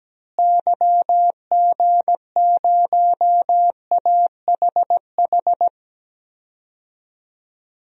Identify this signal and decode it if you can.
Morse code, heard as a keyed tone, about 17 wpm, 700 Hz. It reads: YG0AHH
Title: DXCC-100 Call Signs - 6 Characters - Repeated 3x - 17wpm